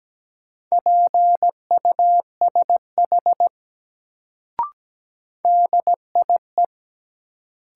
Morse code, heard as a keyed tone, about 17 wpm, 700 Hz. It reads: PUSH DIE
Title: Sentences from Top 1300 Words - New Word Drill Alt - 17wpm